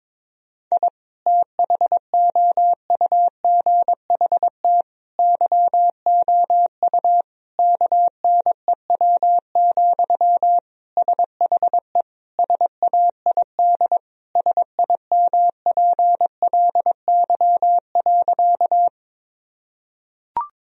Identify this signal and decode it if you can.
Morse code, heard as a keyed tone, about 22 wpm, 700 Hz. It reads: I THOUGHT YOU KNEW, SHE SAID SIMPLY.
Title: Sentences from Top 1000 Words - Review Part 4 - 22wpm